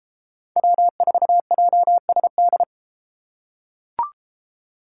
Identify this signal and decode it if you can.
Morse code, heard as a keyed tone, about 32 wpm, 700 Hz. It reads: W4JSD